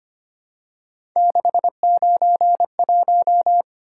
Morse code, heard as a keyed tone, about 25 wpm, 700 Hz. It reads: 691
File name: Three Letter-Number Combinations - 25wpm